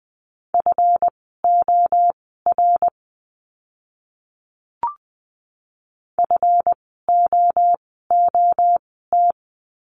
Morse code, heard as a keyed tone, about 20 wpm, spaced as for 14 wpm, 700 Hz. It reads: FOR FOOT
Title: 139 Lesson 23f - Letter f - Words 20wpm